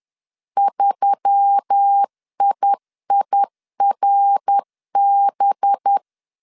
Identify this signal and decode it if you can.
Morse code, heard as a keyed tone, about 10 wpm, 800 Hz. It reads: 3IIRB
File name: CW / DET / 1 9 1500C S B 10wpm 4439